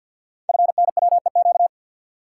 Morse code, heard as a keyed tone, about 48 wpm, 700 Hz. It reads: FNWEQ
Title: Code Groups - 5 Letters - 50wpm